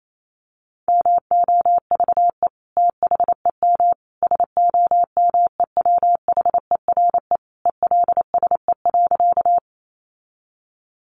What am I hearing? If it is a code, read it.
Morse code, heard as a keyed tone, about 28 wpm, 700 Hz. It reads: MOVE THEM SOMEWHERE ELSE.
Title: Sentences from Top 1400 Words 28wpm